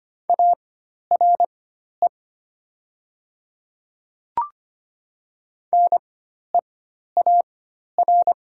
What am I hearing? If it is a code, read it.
Morse code, heard as a keyed tone, about 25 wpm, spaced as for 12 wpm, 700 Hz. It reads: ARE NEAR
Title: 4x Character Spacing - Top 200 Words - 25wpm